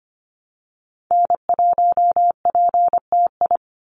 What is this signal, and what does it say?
Morse code, heard as a keyed tone, about 25 wpm, 700 Hz. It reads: N1PTI
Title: USA Call Signs 25wpm